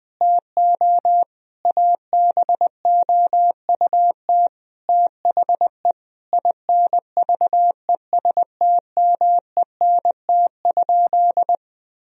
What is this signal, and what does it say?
Morse code, heard as a keyed tone, about 20 wpm, 700 Hz. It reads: TO ABOUT THE INVESTMENT?
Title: Sentences from Top 1000 Words - Review Part 4 - 20wpm